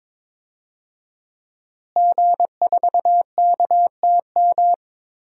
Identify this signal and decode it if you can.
Morse code, heard as a keyed tone, about 22 wpm, 700 Hz. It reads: G4KTM
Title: DXCC-100 Call Signs - 5 Characters - Repeated 3x - 22wpm